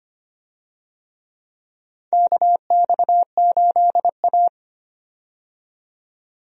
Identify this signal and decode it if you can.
Morse code, heard as a keyed tone, about 25 wpm, 700 Hz. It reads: KX8A
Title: USA Call Signs - 4 Characters - 25wpm